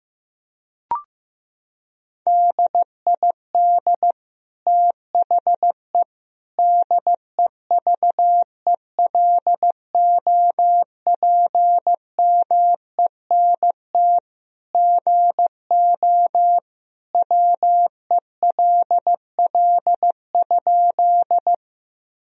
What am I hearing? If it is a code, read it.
Morse code, heard as a keyed tone, about 15 wpm, 700 Hz. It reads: DID THE DEVELOPMENT GO WELL?
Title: Sentences from Top 1000 Words - Review Part 4 - 15wpm